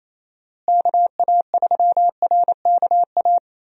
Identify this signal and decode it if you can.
Morse code, heard as a keyed tone, about 28 wpm, 700 Hz. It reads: KA3RKA
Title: USA Call Signs - 6 Characters - Repeated 2x - 28wpm